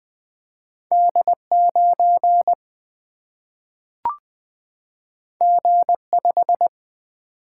Morse code, heard as a keyed tone, about 20 wpm, 700 Hz. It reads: D 9 G 5
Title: Single-Letter Single-Number 20wpm